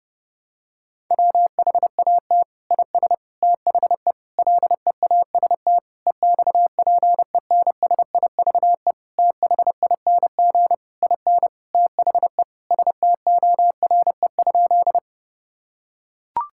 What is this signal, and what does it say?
Morse code, heard as a keyed tone, about 30 wpm, 700 Hz. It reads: WHAT IS THE LEAST EXPENSIVE THING IN THE STORE?
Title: Sentences from Top 1000 Words - Review Part 6 - 30wpm